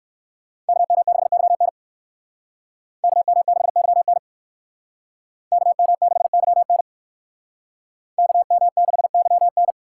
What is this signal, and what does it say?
Morse code, heard as a keyed tone, about 65 wpm, 700 Hz. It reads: KM6YN KM6YN KM6YN KM6YN